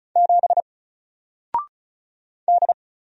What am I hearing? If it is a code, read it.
Morse code, heard as a keyed tone, about 36 wpm, 700 Hz. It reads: 7 D
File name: Single-Letter-Number-35wpm